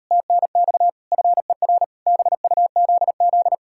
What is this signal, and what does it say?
Morse code, heard as a keyed tone, about 38 wpm, 700 Hz. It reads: TNX FER BUZZ